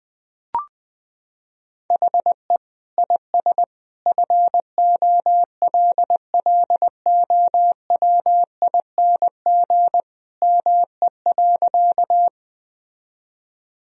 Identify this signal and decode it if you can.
Morse code, heard as a keyed tone, about 20 wpm, 700 Hz. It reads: HE IS FOLLOWING ME.
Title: Sentences from Top 1000 Words - Review Part 4 - 20wpm